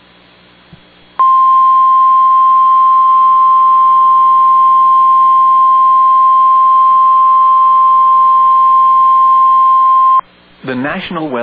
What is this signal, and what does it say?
Emergency Alert System (SAME) tones (the 1050 Hz attention tone), with speech. It is heard from 1.1 to 10.2 s.